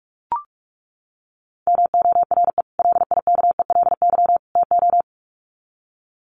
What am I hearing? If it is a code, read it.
Morse code, heard as a keyed tone, about 44 wpm, 700 Hz. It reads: MORE LIKELY TO
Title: Sets of 3 Words 45wpm